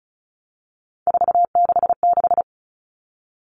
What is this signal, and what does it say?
Morse code, heard as a keyed tone, about 34 wpm, 700 Hz. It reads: 466
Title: Numbers - 3-Digit 35wpm